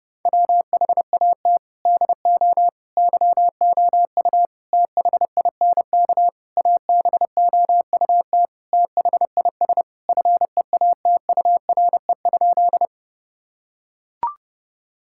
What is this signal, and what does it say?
Morse code, heard as a keyed tone, about 30 wpm, 700 Hz. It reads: WHAT DO YOU THINK ABOUT THIS FEATURE?
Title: Sentences from Top 1000 Words - Review Part 5 - 30wpm